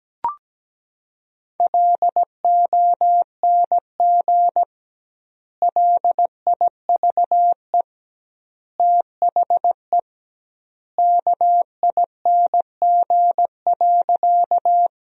Morse code, heard as a keyed tone, about 17 wpm, 700 Hz. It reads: LONG LIVE THE KING.
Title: 2x Word-Spacing - Sentences from Top 400 Words - 17wpm